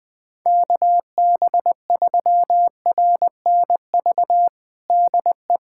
Morse code, heard as a keyed tone, about 20 wpm, 700 Hz. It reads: KB3RNV DE
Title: USA Callsign DE Callsign 20wpm